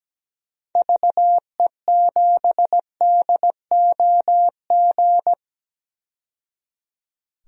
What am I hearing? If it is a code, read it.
Morse code, heard as a keyed tone, about 17 wpm, 700 Hz. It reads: VE7DOG
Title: DXCC-100 Call Signs - 6 Characters - 17wpm